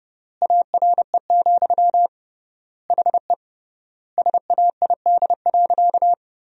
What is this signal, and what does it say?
Morse code, heard as a keyed tone, about 30 wpm, 700 Hz. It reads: ARE, HE SAID.
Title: 3x Word-Spacing - Sentences from Top 100 Words - 30wpm